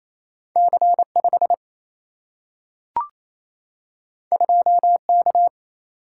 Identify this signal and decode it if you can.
Morse code, heard as a keyed tone, about 28 wpm, 700 Hz. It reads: C5 2K
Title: Single-Letter Single-Number 28wpm